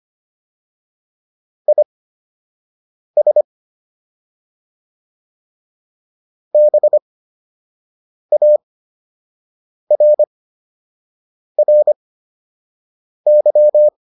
Morse code, heard as a keyed tone, about 25 wpm, spaced as for 6 wpm, 600 Hz. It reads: IS BARRY